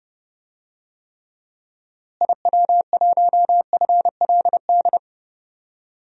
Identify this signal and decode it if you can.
Morse code, heard as a keyed tone, about 30 wpm, 700 Hz. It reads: IW1FLD
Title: DXCC-100 Call Signs - 6 Characters - Repeated 3x - 30wpm